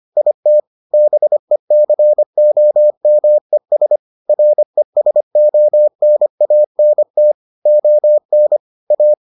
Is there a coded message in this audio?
Yes: Morse code from 0.2 to 9.1 s, heard as a keyed tone, about 25 wpm, 600 Hz. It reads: IT BECOMES RESONANT ON A